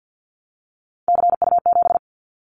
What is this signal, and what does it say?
Morse code, heard as a keyed tone, about 50 wpm, 700 Hz. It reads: CU7
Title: Callsign Prefixes for Top 50 DXCC - 50wpm